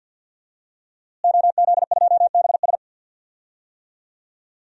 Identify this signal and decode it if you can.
Morse code, heard as a keyed tone, about 46 wpm, 700 Hz. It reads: OZ2BS
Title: DXCC-100 Call Signs - 5 Characters - Repeated 2x - 50wpm